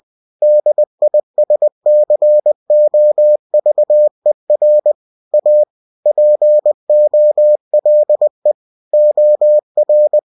Morse code, heard as a keyed tone, about 20 wpm, 600 Hz. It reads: DISCOVER A POLE OR